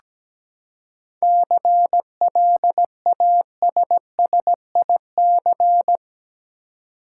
Morse code, heard as a keyed tone, about 17 wpm, 700 Hz. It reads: CLASSIC
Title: Seven Letter Words - 17wpm